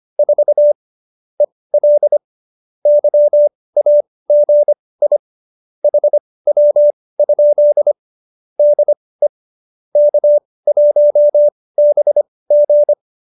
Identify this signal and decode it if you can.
Morse code, heard as a keyed tone, about 25 wpm, spaced as for 18 wpm, 570 Hz. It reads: 4 EL YAGI HW? DE K1BG
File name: qso201b 18